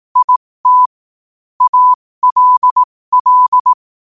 Morse code, heard as a keyed tone, about 18 wpm, spaced as for 15 wpm, 1000 Hz. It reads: IT ALL
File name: qotdfiles / 15wpm